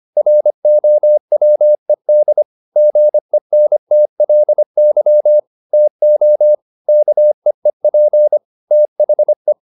Morse code, heard as a keyed tone, about 25 wpm, 600 Hz. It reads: ROWED GENTLY TO KEEP THE
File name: the old man and the sea part 1 25wpm